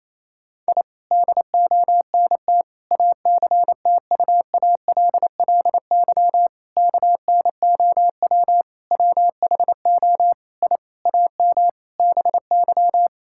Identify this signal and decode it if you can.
Morse code, heard as a keyed tone, about 28 wpm, 700 Hz. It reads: I DONT ACTUALLY KNOW WHO I AM BY
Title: Sentences from Top 1000 Words - Review Part 5 - 28wpm